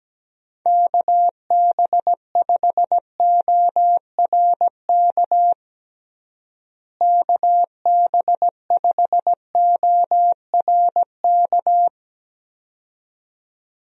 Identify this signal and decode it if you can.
Morse code, heard as a keyed tone, about 17 wpm, 700 Hz. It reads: KB5ORK KB5ORK
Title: USA Call Signs - 6 Characters - Repeated 2x - 17wpm